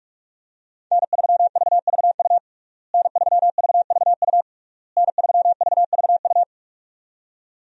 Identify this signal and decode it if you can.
Morse code, heard as a keyed tone, about 46 wpm, 700 Hz. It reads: N3VVU N3VVU N3VVU
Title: USA Call Signs - 5 Characters - Repeated 3x - 45wpm